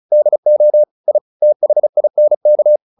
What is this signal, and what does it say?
Morse code, heard as a keyed tone, about 34 wpm, 600 Hz. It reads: DO I THINK
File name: WD402 35